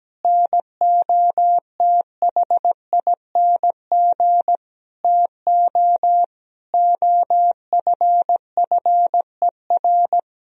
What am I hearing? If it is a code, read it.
Morse code, heard as a keyed tone, about 17 wpm, 700 Hz. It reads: NOTHING TO OFFER
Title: Sentences from Top 1000 Words - Review Part 4 - 17wpm